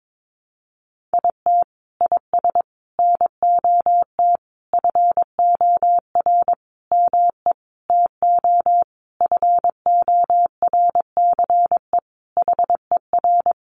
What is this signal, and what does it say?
Morse code, heard as a keyed tone, about 22 wpm, 700 Hz. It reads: IT IS NOT FOR ME TO FORCE HER
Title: Sentences from Top 1500 Words 22wpm